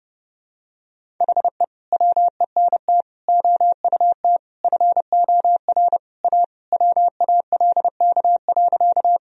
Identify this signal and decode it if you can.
Morse code, heard as a keyed tone, about 32 wpm, 700 Hz. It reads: HE WENT OUT FOR A WALK.
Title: Sentences from Top 1000 Words - Review Part 1 - 30wpm